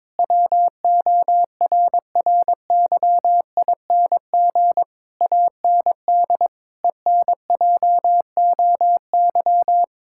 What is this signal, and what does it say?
Morse code, heard as a keyed tone, about 22 wpm, 700 Hz. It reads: WORRYING AND ENJOY
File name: practice / 22wpm / Sentences with 8-Word Phrases - 22wpm